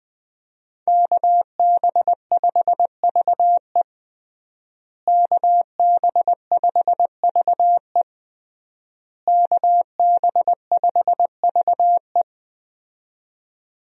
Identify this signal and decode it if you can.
Morse code, heard as a keyed tone, about 21 wpm, 700 Hz. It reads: KB5VE KB5VE KB5VE
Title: USA Call Signs - 5 Characters - Repeated 3x - 20wpm